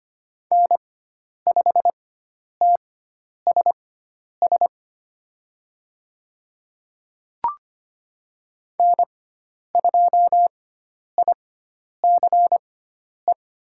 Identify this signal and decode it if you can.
Morse code, heard as a keyed tone, about 25 wpm, spaced as for 10 wpm, 700 Hz. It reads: N5TSS N2ICE
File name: Lesson 16g - Letter c - Call Signs 25wpm